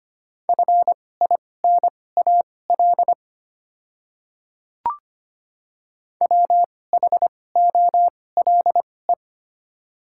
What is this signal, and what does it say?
Morse code, heard as a keyed tone, about 25 wpm, spaced as for 18 wpm, 700 Hz. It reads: FINAL WHOLE
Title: 2x Character Spacing - Top 500 Words - 25wpm